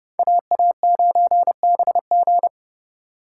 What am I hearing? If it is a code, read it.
Morse code, heard as a keyed tone, about 38 wpm, 700 Hz. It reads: AA9BG